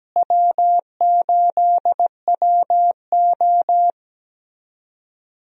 Morse code, heard as a keyed tone, about 17 wpm, 700 Hz. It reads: W8WO
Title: USA Call Signs - 4 Characters - Repeated 2x - 17wpm